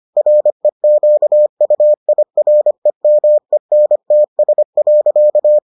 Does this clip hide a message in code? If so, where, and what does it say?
Yes: Morse code from 0.2 to 5.6 s, heard as a keyed tone, about 25 wpm, 600 Hz. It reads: REQUIREMENTS.